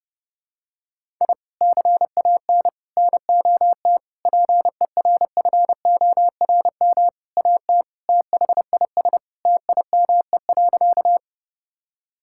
Morse code, heard as a keyed tone, about 32 wpm, 700 Hz. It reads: I CAN NOT PERFORM AT THIS TIME.
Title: Sentences from Top 1000 Words - Review Part 5 - 30wpm